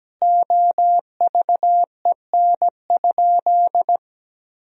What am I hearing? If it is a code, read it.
Morse code, heard as a keyed tone, about 17 wpm, 700 Hz. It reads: OVEN?
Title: Sentences from Top 1000 Words - Review Part 4 - 17wpm